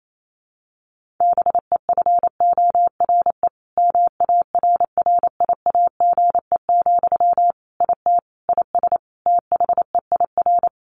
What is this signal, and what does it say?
Morse code, heard as a keyed tone, about 28 wpm, 700 Hz. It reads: BEFORE MARRIAGE, IT IS THEIR
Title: Sentences from Top 1500 Words 28wpm